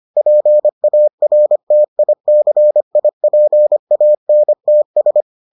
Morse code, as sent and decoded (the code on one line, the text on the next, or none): .--. .- .-. - .. -.-. .. .--. .- -. - ...
PARTICIPANTS